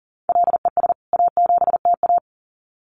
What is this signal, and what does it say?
Morse code, heard as a keyed tone, about 40 wpm, 700 Hz. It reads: LES A7TA
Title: CWT Exchange 40wpm